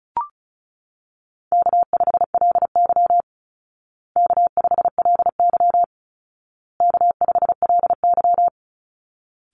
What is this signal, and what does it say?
Morse code, heard as a keyed tone, about 34 wpm, 700 Hz. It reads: K5LY K5LY K5LY
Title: Call Signs Repeated 2x - 35wpm